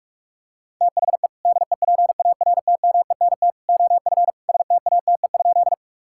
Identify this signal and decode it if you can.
Morse code, heard as a keyed tone, about 42 wpm, 700 Hz. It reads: THE DEPARTMENT OF STATE?
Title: Sentences from Top 1000 Words - Review Part 5 - 45wpm